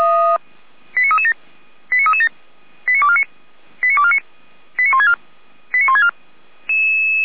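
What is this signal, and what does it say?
Five-tone selective call (ZVEI), heard as quick runs of five tones, tones of about 70 ms. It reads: 89337 89337 89249 89249 89164 89164
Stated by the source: ZVEI five-tone selective call 89250 received by SDR, fire brigade Orlamünde (feuerwehr-orlamuende).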